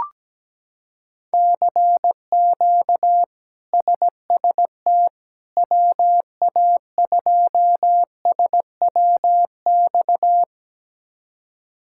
Morse code, heard as a keyed tone, about 17 wpm, 700 Hz. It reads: CQ SST WA2SWX
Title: K1USN SST QSO Practice - 17wpm